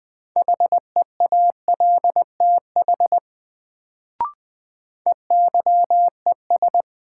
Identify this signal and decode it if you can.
Morse code, heard as a keyed tone, about 20 wpm, 700 Hz. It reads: HEALTH EYES
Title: Sets of 5 Words - New Word Drill - 20wpm